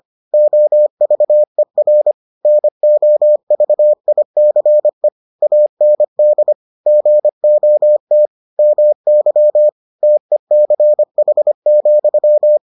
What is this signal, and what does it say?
Morse code, heard as a keyed tone, about 25 wpm, 600 Hz. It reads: OVER NOVICE AND GOT MY TECH,